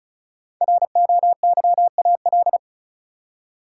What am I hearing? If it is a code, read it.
Morse code, heard as a keyed tone, about 36 wpm, 700 Hz. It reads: ROYAL
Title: Five Letter Words - 35wpm